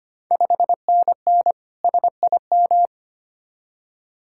Morse code, heard as a keyed tone, about 25 wpm, 700 Hz. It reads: 5NN SIM
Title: Ontario QSO Party Exchange 25wpm